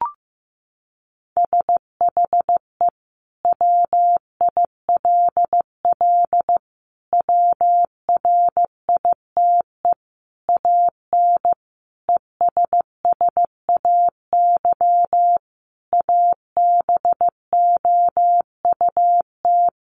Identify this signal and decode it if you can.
Morse code, heard as a keyed tone, about 15 wpm, 700 Hz. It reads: SHE WILL WRITE AN ESSAY ABOUT
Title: Sentences from Top 1500 Words 15wpm